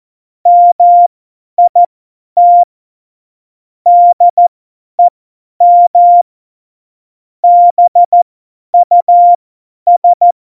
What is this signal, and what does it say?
Morse code, heard as a keyed tone, about 14 wpm, spaced as for 10 wpm, 700 Hz. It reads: MIT DEM BUS
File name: Morsekurs / Witze / Witz 27. November 2018, 50 F70